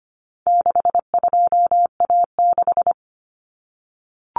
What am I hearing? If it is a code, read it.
Morse code, heard as a keyed tone, about 25 wpm, 700 Hz. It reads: 62A6
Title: Common Cut Numbers - 4-Digit 25wpm